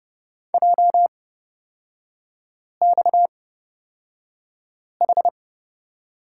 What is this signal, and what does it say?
Morse code, heard as a keyed tone, about 30 wpm, spaced as for 5 wpm, 700 Hz. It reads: JXH
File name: Lesson 37e - Letter x - Three Characters 30wpm